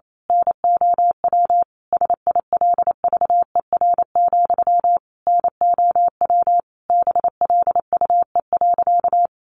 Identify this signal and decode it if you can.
Morse code, heard as a keyed tone, about 28 wpm, 700 Hz. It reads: NOW SILVER, NOW BLUE.
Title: A Sound Of Thunder 28wpm